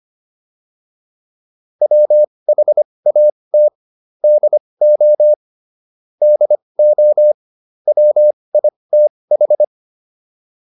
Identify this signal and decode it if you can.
Morse code, heard as a keyed tone, about 25 wpm, spaced as for 20 wpm, 600 Hz. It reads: WHAT DO DO WITH